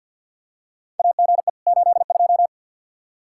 Morse code, heard as a keyed tone, about 52 wpm, 700 Hz. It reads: AGE 82
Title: K1USN SST QSO Practice 50wpm